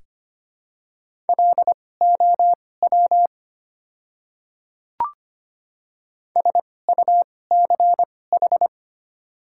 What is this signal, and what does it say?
Morse code, heard as a keyed tone, about 25 wpm, spaced as for 18 wpm, 700 Hz. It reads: LOW SUCH
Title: 2x Character Spacing - Top 300 Words - 25wpm